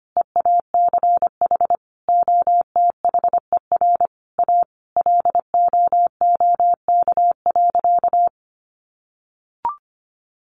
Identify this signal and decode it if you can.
Morse code, heard as a keyed tone, about 25 wpm, 700 Hz. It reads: EACH OTHER A LOOK.
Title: A Sound Of Thunder 25wpm